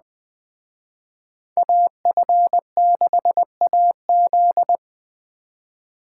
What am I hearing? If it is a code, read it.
Morse code, heard as a keyed tone, about 19 wpm, 700 Hz. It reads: AF6AZ